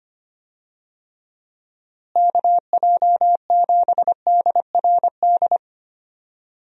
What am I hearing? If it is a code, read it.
Morse code, heard as a keyed tone, about 25 wpm, 700 Hz. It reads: KJ7DRD
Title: USA Call Signs - 6 Characters - Repeated 2x - 25wpm